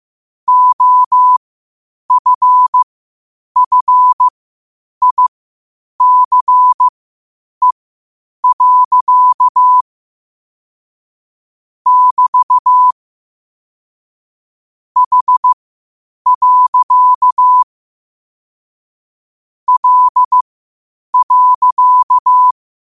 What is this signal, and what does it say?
Morse code, heard as a keyed tone, about 15 wpm, spaced as for 8 wpm, 1000 Hz. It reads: OFFICE. <BT> H. L.
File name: qotdfiles / 8wpm